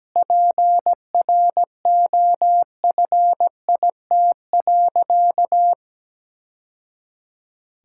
Morse code, heard as a keyed tone, about 17 wpm, 700 Hz. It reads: PROFIT.
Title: Sentences from Top 1000 Words - Review Part 5 - 17wpm